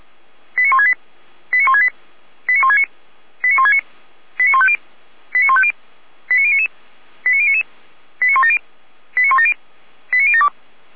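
Five-tone selective call (ZVEI), heard as quick runs of five tones, tones of about 70 ms. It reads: 89168 89168 89169 89169 89150 89150 89099 89099 89170 89170 89072
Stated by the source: ZVEI five-tone selective call 89220 received on SDR, from feuerwehr-orlamuende fire brigade server.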